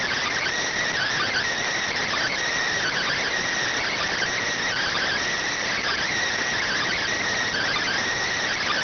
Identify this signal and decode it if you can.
Emergency Alert System (SAME) tones (header bursts), heard as harsh buzzing bursts. It is heard from 0.9 to 8.2 s.